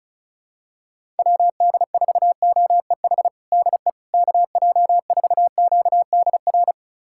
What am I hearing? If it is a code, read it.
Morse code, heard as a keyed tone, about 36 wpm, 700 Hz. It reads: WD4OEH DE KJ4QDR
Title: USA Callsign DE Callsign - 6 Characters - 35wpm